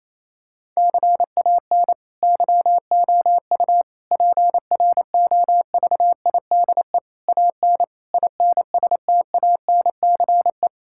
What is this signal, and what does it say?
Morse code, heard as a keyed tone, about 28 wpm, 700 Hz. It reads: CAN YOU PROVIDE AN INSTANCE
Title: Sentences from Top 1000 Words - Review Part 5 - 28wpm